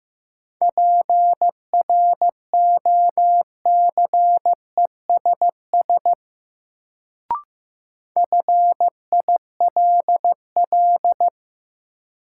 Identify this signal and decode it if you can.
Morse code, heard as a keyed tone, about 15 wpm, 700 Hz. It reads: PROCESS FILL
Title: Sentences from Top 1000 Words - Review Part 3 - New Word Drill - 15wpm